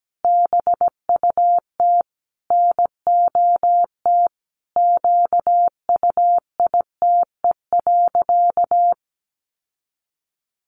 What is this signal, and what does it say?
Morse code, heard as a keyed tone, about 17 wpm, 700 Hz. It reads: BUT NOT QUITE.
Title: Sentences from Top 1300 Words 17wpm